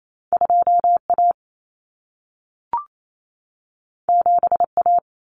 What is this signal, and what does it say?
Morse code, heard as a keyed tone, about 28 wpm, 700 Hz. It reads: 2A 7A
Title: Common Cut Numbers - 2-Digit 28wpm